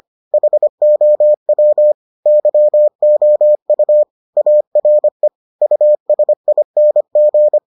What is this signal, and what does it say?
Morse code, heard as a keyed tone, about 25 wpm, 600 Hz. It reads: HOW YOU ARE USING